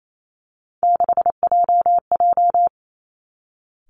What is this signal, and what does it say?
Morse code, heard as a keyed tone, about 28 wpm, 700 Hz. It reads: 6JJ
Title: Three Challenging Letter-Number Combinations 28wpm